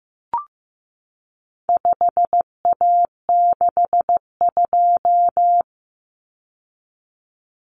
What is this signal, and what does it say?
Morse code, heard as a keyed tone, about 15 wpm, 700 Hz. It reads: 5A62
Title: Common Cut Numbers - 4-Digit 15wpm